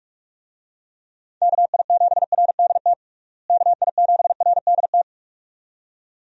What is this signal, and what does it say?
Morse code, heard as a keyed tone, about 46 wpm, 700 Hz. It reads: KI7RDT KI7RDT
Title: USA Call Signs - 6 Characters - Repeated 2x - 45wpm